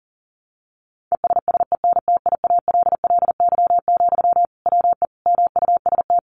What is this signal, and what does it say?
Morse code, heard as a keyed tone, about 40 wpm, 700 Hz. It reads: ESSENTIALLY, WE MUST
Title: Sentences from Top 1300 Words 40wpm